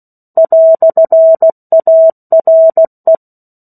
Morse code, heard as a keyed tone, about 16 wpm, 650 Hz. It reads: "ARE
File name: 03-Chapter-03 16 WPM